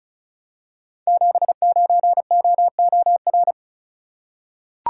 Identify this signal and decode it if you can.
Morse code, heard as a keyed tone, about 36 wpm, 700 Hz. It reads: 79OOR
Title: Code Groups - 5 Letter:Numbers - 35wpm